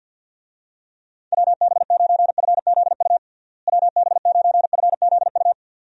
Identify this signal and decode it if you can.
Morse code, heard as a keyed tone, about 46 wpm, 700 Hz. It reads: WB9FZU WB9FZU
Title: USA Call Signs - 6 Characters - Repeated 2x - 50wpm